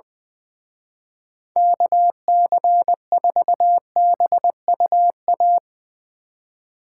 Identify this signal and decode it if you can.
Morse code, heard as a keyed tone, about 21 wpm, 700 Hz. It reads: KC4BUA
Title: Speed Racing - USA 2x3 Call Signs 20wpm